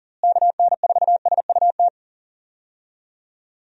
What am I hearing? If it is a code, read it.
Morse code, heard as a keyed tone, about 40 wpm, 700 Hz. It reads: KN4SUT
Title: USA Call Signs - 6 Characters - Repeated 2x - 40wpm